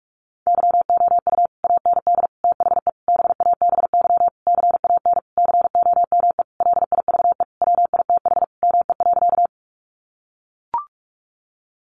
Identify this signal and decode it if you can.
Morse code, heard as a keyed tone, about 44 wpm, 700 Hz. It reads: YOU AND THE BABY CAN COME LIVE WITH ME.
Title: Sentences from Top 1300 Words 45wpm